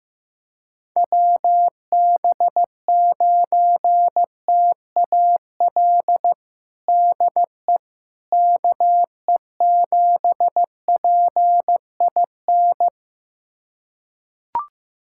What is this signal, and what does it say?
Morse code, heard as a keyed tone, about 15 wpm, 700 Hz. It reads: WB9TAL DE KE7PIN
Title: USA Callsign DE Callsign - 6 Characters - 15wpm